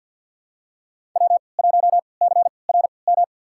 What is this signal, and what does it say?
Morse code, heard as a keyed tone, about 50 wpm, spaced as for 28 wpm, 700 Hz. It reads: W1CRM
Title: USA Call Signs - 3x Character Spacing - 50wpm